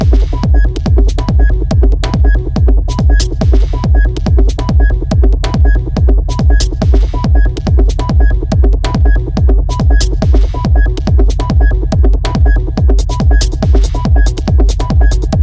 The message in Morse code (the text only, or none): none